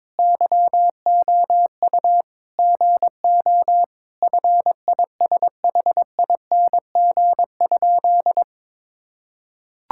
Morse code, heard as a keyed tone, about 22 wpm, 700 Hz. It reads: YOU GO FISHING?
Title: Sentences from Top 1000 Words - Review Part 5 - 22wpm